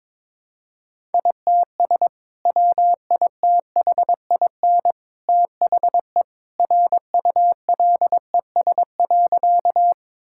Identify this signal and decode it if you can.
Morse code, heard as a keyed tone, about 21 wpm, 700 Hz. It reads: ITS WITHIN THE RULES.
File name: practice / 22wpm / Sentences with 2-Word Phrases - 22wpm